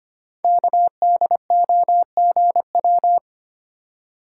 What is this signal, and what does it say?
Morse code, heard as a keyed tone, about 25 wpm, 700 Hz. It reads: KDOGW
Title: Code Groups - 5 Letters - 25wpm